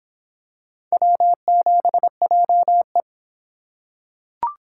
Morse code, heard as a keyed tone, about 23 wpm, 700 Hz. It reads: W7JE